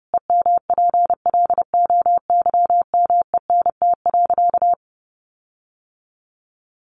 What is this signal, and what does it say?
Morse code, heard as a keyed tone, about 30 wpm, 700 Hz. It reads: EMPLOYMENT.
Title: Sentences from Top 1500 Words 30wpm